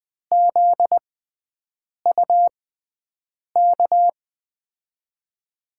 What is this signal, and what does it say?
Morse code, heard as a keyed tone, about 20 wpm, spaced as for 7 wpm, 700 Hz. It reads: ZUK
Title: USA Call Signs - 6 Characters - 6x Character Spacing - 20wpm